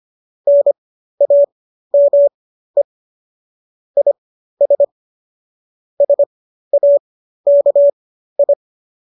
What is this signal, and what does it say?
Morse code, heard as a keyed tone, about 25 wpm, spaced as for 13 wpm, 570 Hz. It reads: NAME IS SAKI